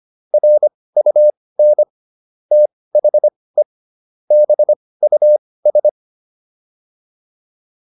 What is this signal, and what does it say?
Morse code, heard as a keyed tone, about 25 wpm, spaced as for 18 wpm, 600 Hz. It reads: RUN THE BUS
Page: intermediate-practice-files